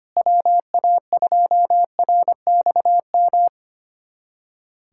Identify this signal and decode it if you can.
Morse code, heard as a keyed tone, about 25 wpm, 700 Hz. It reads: WA2RXM
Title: USA Callsign DE Callsign 25wpm